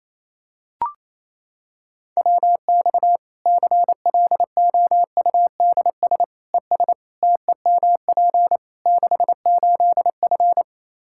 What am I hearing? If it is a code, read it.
Morse code, heard as a keyed tone, about 28 wpm, 700 Hz. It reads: WX CLOUDS ES TEMP 68F
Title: Practice QSO - Exchange 2 - Info - 28wpm